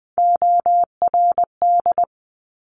Morse code, heard as a keyed tone, about 20 wpm, 700 Hz. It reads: ORD
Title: Top Three Letter Combinations in Words 20wpm